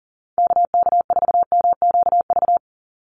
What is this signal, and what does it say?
Morse code, heard as a keyed tone, about 40 wpm, 700 Hz. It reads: KK4MQV
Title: USA Call Signs 40wpm